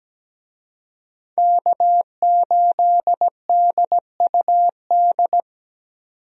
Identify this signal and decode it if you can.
Morse code, heard as a keyed tone, about 17 wpm, 700 Hz. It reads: K8DUD
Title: USA Call Signs - 5 Characters - 17wpm